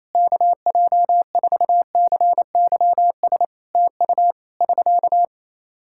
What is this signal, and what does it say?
Morse code, heard as a keyed tone, about 28 wpm, 700 Hz. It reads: KJ4CYS TU <SK>
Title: K1USN SST QSO Practice 28wpm